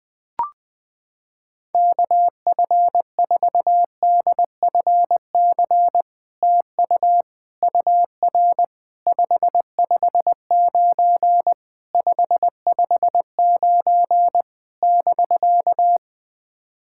Morse code, heard as a keyed tone, about 20 wpm, 700 Hz. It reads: KF4DFC TU UR 559 559 <BK>
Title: SOTA QSO Practice 20wpm